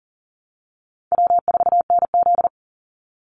Morse code, heard as a keyed tone, about 40 wpm, 700 Hz. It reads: W4NZ
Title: Easy Call Signs Repeated 2x - 40wpm